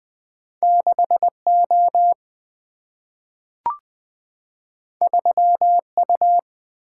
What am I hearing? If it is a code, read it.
Morse code, heard as a keyed tone, about 20 wpm, 700 Hz. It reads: 6 O 3 U
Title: Single-Letter Single-Number 20wpm